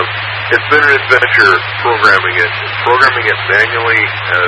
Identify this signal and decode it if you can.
Emergency Alert System (SAME) tones, heard as harsh buzzing bursts, with speech. It is heard from 1.1 to 3.7 s.